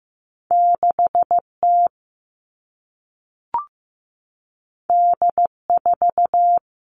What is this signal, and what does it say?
Morse code, heard as a keyed tone, about 15 wpm, 700 Hz. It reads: 6T D4
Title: All Cut Numbers - 2-Digit 15wpm